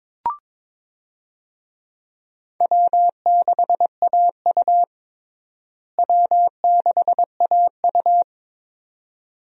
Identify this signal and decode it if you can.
Morse code, heard as a keyed tone, about 22 wpm, 700 Hz. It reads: W6AU W6AU
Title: USA Call Signs - 4 Characters - Repeated 2x - 22wpm